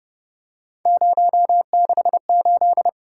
Morse code, heard as a keyed tone, about 31 wpm, 700 Hz. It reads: 068